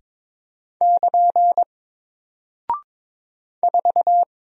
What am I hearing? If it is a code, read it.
Morse code, heard as a keyed tone, about 22 wpm, 700 Hz. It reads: <KN> 4
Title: Single Character (Letter:Number:Punctuation:Prosign) 22wpm